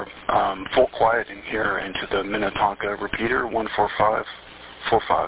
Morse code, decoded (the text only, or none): none